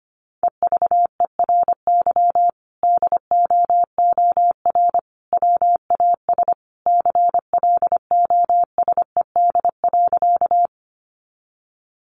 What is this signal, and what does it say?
Morse code, heard as a keyed tone, about 25 wpm, 700 Hz. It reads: EVERY DOOR WAS CLOSED.
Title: Sentences from Top 1400 Words 25wpm